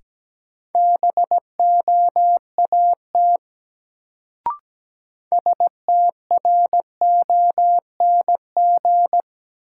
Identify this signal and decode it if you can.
Morse code, heard as a keyed tone, about 17 wpm, 700 Hz. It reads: BOAT STRONG
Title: Sentences from Top 500 Words - New Word Drill - 17wpm